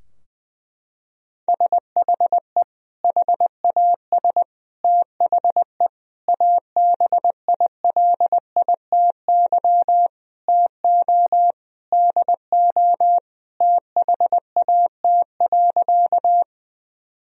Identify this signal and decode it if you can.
Morse code, heard as a keyed tone, about 20 wpm, 700 Hz. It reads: SHE HAS THE ABILITY TO DO THAT.
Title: Sentences from Top 1000 Words - Review Part 3 - 20wpm